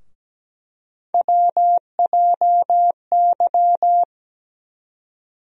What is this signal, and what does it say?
Morse code, heard as a keyed tone, about 17 wpm, 700 Hz. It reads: WJY
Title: USA Call Sign Postfixes - 3 Letters 17wpm